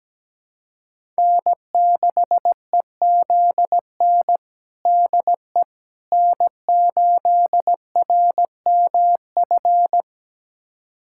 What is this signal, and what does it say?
Morse code, heard as a keyed tone, about 17 wpm, 700 Hz. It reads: N6EZN DE N8RMF
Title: USA Callsign DE Callsign - 5 Characters - 17wpm